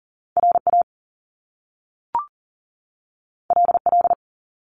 Morse code, heard as a keyed tone, about 40 wpm, 700 Hz. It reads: RA LL
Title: Top Two Letter Combinations in Words 40wpm